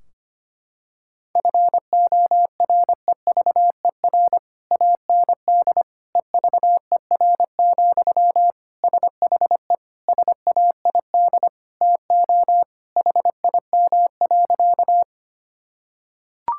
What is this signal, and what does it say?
Morse code, heard as a keyed tone, about 25 wpm, 700 Hz. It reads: FOREVER AND EVER, SHE SAID TO HIM.
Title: Sentences from Top 1600 Words 25wpm